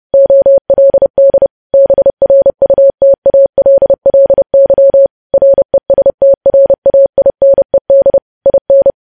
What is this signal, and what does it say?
Morse code, heard as a keyed tone, about 30 wpm, 550 Hz. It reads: OLD BRUTALLY RESTRAINED IN
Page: news headlines in morse code at 30 wpm